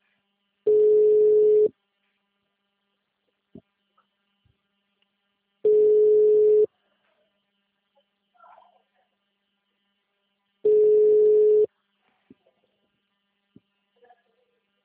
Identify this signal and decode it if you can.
Ringback tone (Europe, 425 Hz).
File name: out- -2005- - - .37159